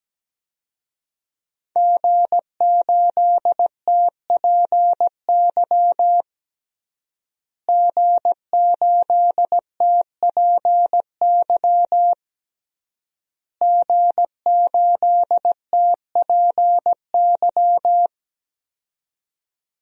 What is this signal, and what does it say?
Morse code, heard as a keyed tone, about 17 wpm, 700 Hz. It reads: G8TPY G8TPY G8TPY
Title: DXCC-100 Call Signs - 5 Characters - Repeated 3x - 17wpm